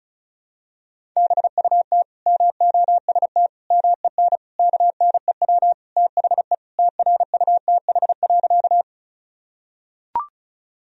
Morse code, heard as a keyed tone, about 34 wpm, 700 Hz. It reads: BUT MOST MEN KNEW THE TRUTH.
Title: Sentences from Top 1000 Words - Review Part 1 - 35wpm